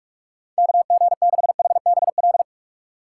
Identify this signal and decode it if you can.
Morse code, heard as a keyed tone, about 46 wpm, 700 Hz. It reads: KG6HBL
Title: USA Call Signs - 2x Character Spacing - 45wpm